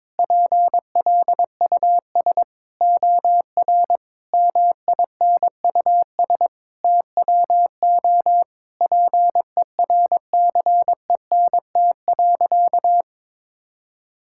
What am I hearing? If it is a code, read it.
Morse code, heard as a keyed tone, about 22 wpm, 700 Hz. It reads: PLUS OR MINUS TWO PERCENT.
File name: practice / 22wpm / Sentences with 8-Word Phrases - 22wpm